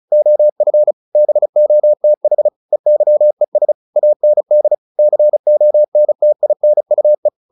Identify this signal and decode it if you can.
Morse code, heard as a keyed tone, about 34 wpm, 600 Hz. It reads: OF BOTH EYES AND CONTINUE
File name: the old man and the sea / the old man and the sea 1 35wpm